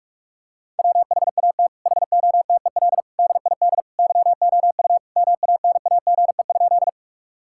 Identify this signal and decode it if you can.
Morse code, heard as a keyed tone, about 46 wpm, 700 Hz. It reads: WHAT HOTEL DID YOU MANAGE?
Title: Sentences from Top 1600 Words 45wpm